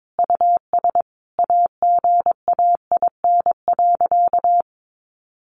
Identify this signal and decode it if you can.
Morse code, heard as a keyed tone, about 22 wpm, 700 Hz. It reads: US AGAIN.
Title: Sentences from Top 1100 Words 22wpm